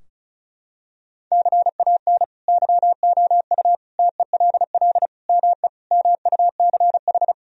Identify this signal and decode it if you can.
Morse code, heard as a keyed tone, about 34 wpm, 700 Hz. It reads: CAN YOU TELL ME MUCH
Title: Sentences from Top 1000 Words - Review Part 4 - 35wpm